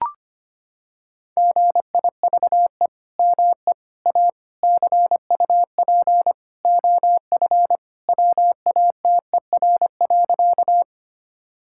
Morse code, heard as a keyed tone, about 25 wpm, 700 Hz. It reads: GIVE ME A CUP OF WATER.
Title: Sentences from Top 1600 Words 25wpm